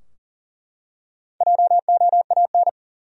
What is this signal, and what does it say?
Morse code, heard as a keyed tone, about 40 wpm, 700 Hz. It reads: JOAN
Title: USA Common Names - 40wpm